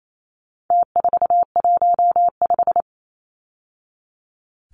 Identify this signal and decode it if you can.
Morse code, heard as a keyed tone, about 28 wpm, 700 Hz. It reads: T415
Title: Numbers - Contest Style - 4-Digit 28wpm